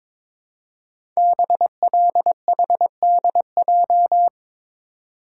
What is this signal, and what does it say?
Morse code, heard as a keyed tone, about 23 wpm, 690 Hz. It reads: BLHDJ
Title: Code Groups - 5 Letters - 22wpm